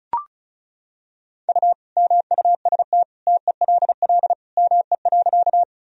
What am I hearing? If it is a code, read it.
Morse code, heard as a keyed tone, about 34 wpm, 700 Hz. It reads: U MUST TELL ME.
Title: Sentences from Top 200 Words and Abbreviations 35wpm